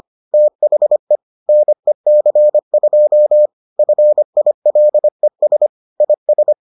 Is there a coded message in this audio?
Yes: Morse code from 0.3 to 6.5 s, heard as a keyed tone, about 25 wpm, 600 Hz. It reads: THE NEC2 FILES IS